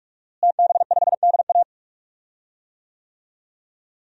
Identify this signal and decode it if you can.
Morse code, heard as a keyed tone, about 46 wpm, 700 Hz. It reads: TB5DA
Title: DXCC-100 Call Signs - 5 Characters - Repeated 2x - 45wpm